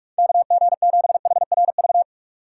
Morse code, heard as a keyed tone, about 45 wpm, 700 Hz. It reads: KG7HRV
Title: USA Call Signs - 6 Characters - 6x Character Spacing - 45wpm